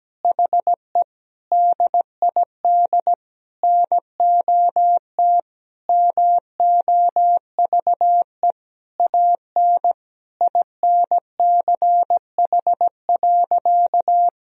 Morse code, heard as a keyed tone, about 17 wpm, 700 Hz. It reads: HE DID NOT MOVE AN INCH.
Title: Sentences from Top 1000 Words - Review Part 3 - 17wpm